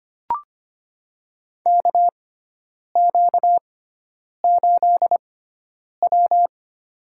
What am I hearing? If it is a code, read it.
Morse code, heard as a keyed tone, about 25 wpm, spaced as for 9 wpm, 700 Hz. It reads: KQ8W
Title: USA Call Signs - 4 Characters - 6x Character Spacing - 25wpm